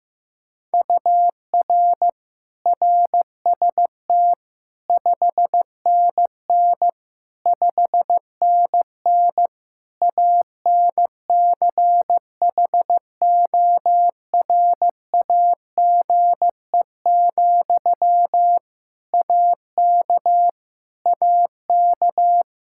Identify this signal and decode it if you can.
Morse code, heard as a keyed tone, about 15 wpm, 700 Hz. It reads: UR RST 5NN 5NN ANCHORAGE, AK AK
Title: Practice QSO - Exchange 1 - Beginning - 15wpm